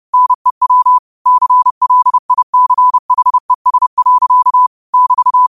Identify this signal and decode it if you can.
Morse code, heard as a keyed tone, about 30 wpm, 1000 Hz. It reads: NEW CLICHES. <BT>
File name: 30wpm